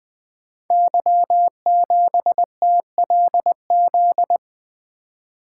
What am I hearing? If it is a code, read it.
Morse code, heard as a keyed tone, about 20 wpm, 700 Hz. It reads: Y7TLZ
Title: Code Groups - 5 Letter:Numbers - 20wpm